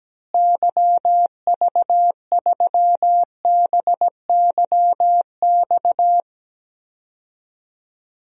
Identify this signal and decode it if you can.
Morse code, heard as a keyed tone, about 17 wpm, 700 Hz. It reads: YV3BYX
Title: DXCC-100 Call Signs - 6 Characters - Repeated 2x - 17wpm